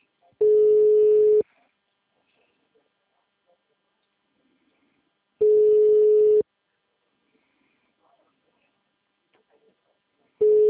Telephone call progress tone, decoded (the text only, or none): ringback tone